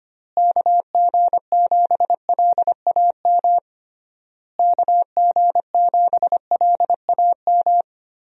Morse code, heard as a keyed tone, about 25 wpm, 700 Hz. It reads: KG7LAM KG7LAM
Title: USA Call Signs - 6 Characters - Repeated 3x - 25wpm